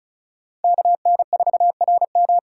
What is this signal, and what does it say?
Morse code, heard as a keyed tone, about 34 wpm, 700 Hz. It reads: KN4RM
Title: USA Call Signs - 3x Character Spacing - 35wpm